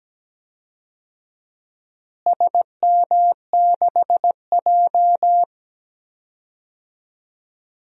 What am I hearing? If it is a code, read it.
Morse code, heard as a keyed tone, about 17 wpm, 700 Hz. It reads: SM6J
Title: DXCC-100 Call Signs - 4 Characters - Repeated 3x - 17wpm